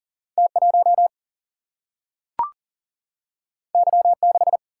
Morse code, heard as a keyed tone, about 40 wpm, 700 Hz. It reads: T 1 Y 6
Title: Single-Letter Single-Number 40wpm